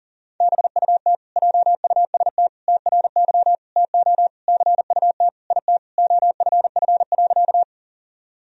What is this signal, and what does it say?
Morse code, heard as a keyed tone, about 40 wpm, 700 Hz. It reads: BUT JUST TRY TO CUT IT OFF.
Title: Sentences from Top 1000 Words - Review Part 1 - 40wpm